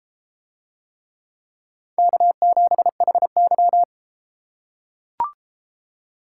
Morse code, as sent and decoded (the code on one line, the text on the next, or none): -.- --... .... -.--
K7HY